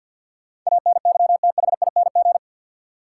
Morse code, heard as a keyed tone, about 46 wpm, 700 Hz. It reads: ANYTHING
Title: Sentences from Top 1000 Words - New Word Drill - 50wpm